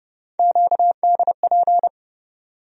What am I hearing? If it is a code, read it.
Morse code, heard as a keyed tone, about 31 wpm, 700 Hz. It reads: QDP
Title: USA Call Sign Postfixes - 3 Letters 30wpm